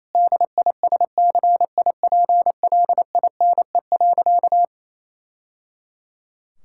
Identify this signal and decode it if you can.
Morse code, heard as a keyed tone, about 27 wpm, 700 Hz. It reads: DISCIPLINE.
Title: Sentences from Top 1000 Words - Review Part 6 - 28wpm